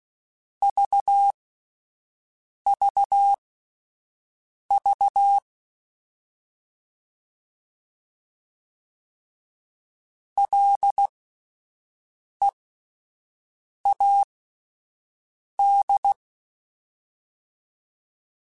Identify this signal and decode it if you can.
Morse code, heard as a keyed tone, about 16 wpm, spaced as for 5 wpm, 800 Hz. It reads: VVV LEAD
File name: Alphabetical / Alphabetical mp3s / Ls @5